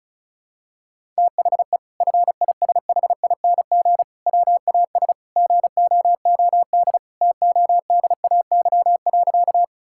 Morse code, heard as a keyed tone, about 36 wpm, 700 Hz. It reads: THE FISHING WAS GOOD TODAY.
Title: Sentences from Top 1000 Words - Review Part 5 - 35wpm